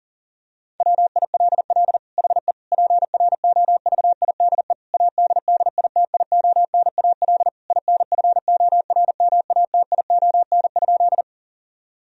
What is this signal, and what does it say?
Morse code, heard as a keyed tone, about 40 wpm, 700 Hz. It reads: WILL HE PROVIDE ADDITIONAL INFORMATION?
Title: Sentences from Top 1000 Words - Review Part 4 - 40wpm